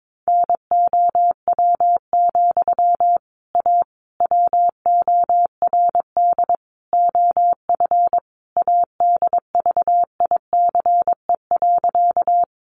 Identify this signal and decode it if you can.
Morse code, heard as a keyed tone, about 22 wpm, 700 Hz. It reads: NOW, A WORD OF ADVICE.
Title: Sentences from Top 1100 Words 22wpm